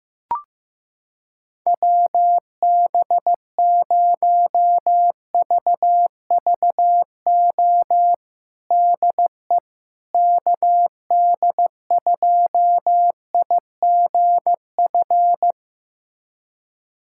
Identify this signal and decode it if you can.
Morse code, heard as a keyed tone, about 15 wpm, 700 Hz. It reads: WB0VVO DE KD2IGF
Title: USA Callsign DE Callsign - 6 Characters - 15wpm